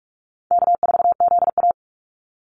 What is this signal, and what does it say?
Morse code, heard as a keyed tone, about 46 wpm, 700 Hz. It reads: K4ZA
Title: Easy Call Signs - 45wpm